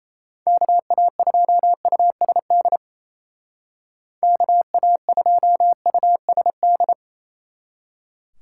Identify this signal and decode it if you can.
Morse code, heard as a keyed tone, about 37 wpm, 700 Hz. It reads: KA2USD KA2USD